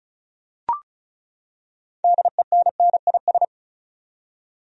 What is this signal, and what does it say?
Morse code, heard as a keyed tone, about 36 wpm, 700 Hz. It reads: DENNIS
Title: USA Common Names - 35wpm